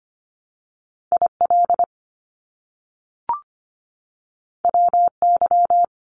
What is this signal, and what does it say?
Morse code, heard as a keyed tone, about 25 wpm, 700 Hz. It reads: IL WY
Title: US States 25wpm